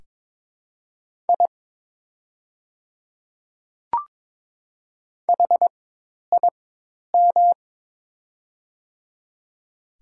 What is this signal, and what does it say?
Morse code, heard as a keyed tone, about 22 wpm, spaced as for 10 wpm, 700 Hz. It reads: I HIM
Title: 4x Character Spacing - Top 100 Words - 22wpm